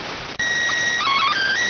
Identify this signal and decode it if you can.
SSTV (slow-scan television), PD 290.